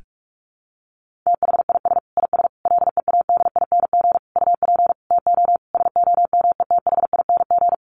Morse code, heard as a keyed tone, about 44 wpm, 700 Hz. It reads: THIS IS LEADING UP TO SOMETHING
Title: Sentences from Top 1100 Words 45wpm